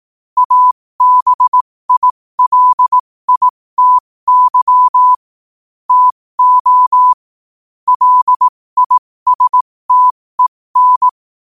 Morse code, heard as a keyed tone, about 18 wpm, spaced as for 15 wpm, 1000 Hz. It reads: ABILITY TO LISTEN
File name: 15wpm